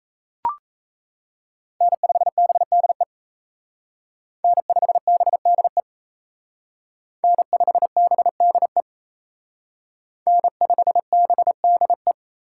Morse code, heard as a keyed tone, about 33 wpm, 700 Hz. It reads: N5BDE N5BDE N5BDE N5BDE
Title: Speed Racing - USA 1x3 Call Signs 28wpm